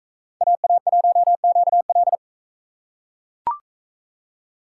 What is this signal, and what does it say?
Morse code, heard as a keyed tone, about 42 wpm, 700 Hz. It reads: AA1QL